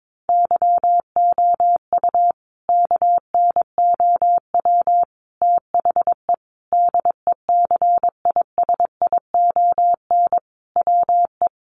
Morse code, heard as a keyed tone, about 22 wpm, 700 Hz. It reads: YOU KNOW THE DECISION WE
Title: Sentences from Top 1000 Words 22wpm